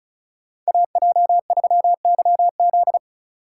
Morse code, heard as a keyed tone, about 39 wpm, 700 Hz. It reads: AJ3YZ